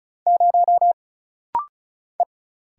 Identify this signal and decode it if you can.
Morse code, heard as a keyed tone, about 36 wpm, 700 Hz. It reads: 0 E
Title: Single Letter-Number - Rapid-Fire 35wpm